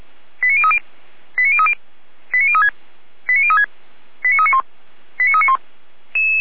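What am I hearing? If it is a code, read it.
Five-tone selective call (ZVEI), heard as quick runs of five tones, tones of about 70 ms. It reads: 89030 89030 89036 89036 89391 89391
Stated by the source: SDR receiver, ZVEI five-tone selective call, address 89502, feuerwehr-orlamuende (fire brigade Orlamünde).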